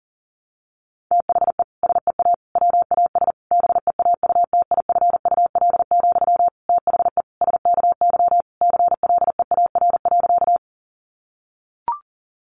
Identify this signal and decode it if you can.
Morse code, heard as a keyed tone, about 40 wpm, 700 Hz. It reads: THE SEA WAS BEAUTIFUL, THE SKY CLEAR.
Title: Sentences from Top 1200 Words 40wpm